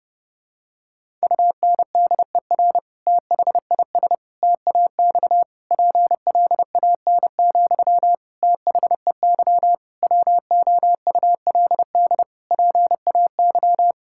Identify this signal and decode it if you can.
Morse code, heard as a keyed tone, about 30 wpm, 700 Hz. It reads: UNDER THIS TAX PLAN, THEY WOULD PAY
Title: Sentences from Top 1000 Words - Review Part 5 - 30wpm